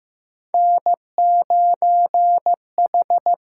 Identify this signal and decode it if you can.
Morse code, heard as a keyed tone, about 15 wpm, 700 Hz. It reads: N 9 H
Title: USA Call Signs - 6x Character Spacing - 15wpm